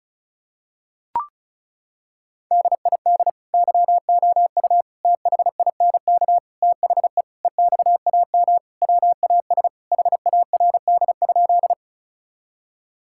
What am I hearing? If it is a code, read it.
Morse code, heard as a keyed tone, about 36 wpm, 700 Hz. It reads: DID YOU THINK THE EXAM WAS HARD?
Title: Sentences from Top 1000 Words - Review Part 5 - 35wpm